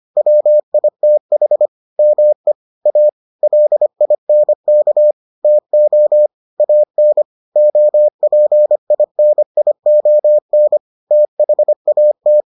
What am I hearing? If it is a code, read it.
Morse code, heard as a keyed tone, about 25 wpm, 600 Hz. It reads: WITH ME A LINK TO AN OPINION THAT